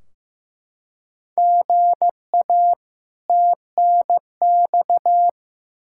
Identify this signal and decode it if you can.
Morse code, heard as a keyed tone, about 15 wpm, 700 Hz. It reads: GA TNX
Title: K1USN SST QSO Practice 15wpm